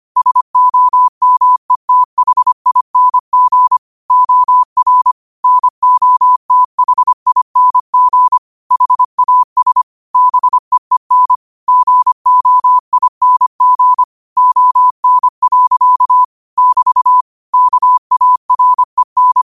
Morse code, heard as a keyed tone, about 25 wpm, 1000 Hz. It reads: SOMETHING OR NOTHING HAS BEEN GOING ON. <BT> KAREN
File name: qotdfiles / 25wpm